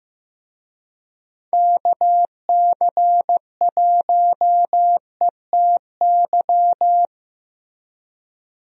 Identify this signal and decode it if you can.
Morse code, heard as a keyed tone, about 15 wpm, 700 Hz. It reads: KC1ETY
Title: USA Call Signs - 6 Characters - Repeated 3x - 15wpm